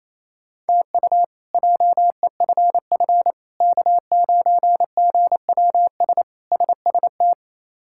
Morse code, heard as a keyed tone, about 28 wpm, 700 Hz. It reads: TU JEFF K9GWS SST
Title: K1USN SST QSO Practice - 28wpm